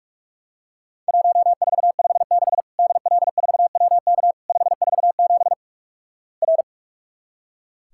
Morse code, heard as a keyed tone, about 46 wpm, 700 Hz. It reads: 1456 DL4WK 547 R
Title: Worked All Europe DX Contest - QTC Exchange - 45wpm